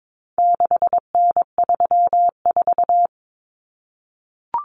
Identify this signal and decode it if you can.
Morse code, heard as a keyed tone, about 22 wpm, 700 Hz. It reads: 6N34
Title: Common Cut Numbers - 4-Digit 22wpm